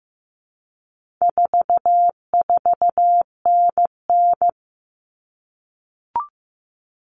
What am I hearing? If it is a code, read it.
Morse code, heard as a keyed tone, about 15 wpm, 700 Hz. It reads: 44NN
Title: Common Cut Numbers - 4-Digit 15wpm